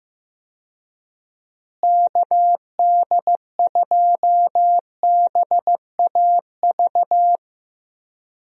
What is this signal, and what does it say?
Morse code, heard as a keyed tone, about 16 wpm, 700 Hz. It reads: KD2BAV
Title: USA Call Signs - 6 Characters - Repeated 3x - 15wpm